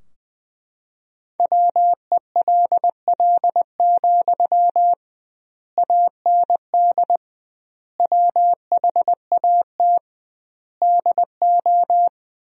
Morse code, heard as a keyed tone, about 21 wpm, 700 Hz. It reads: WELL, AND WHAT DO
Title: 2x Word-Spacing - Sentences from Top 200 Words - 20wpm